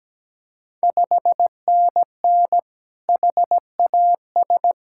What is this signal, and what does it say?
Morse code, heard as a keyed tone, about 17 wpm, 690 Hz. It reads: 5NN HAS
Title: Ontario QSO Party Exchange 17wpm